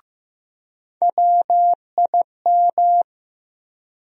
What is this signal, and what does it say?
Morse code, heard as a keyed tone, about 15 wpm, 700 Hz. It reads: WIM
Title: USA Call Sign Postfixes - 3 Letters 15wpm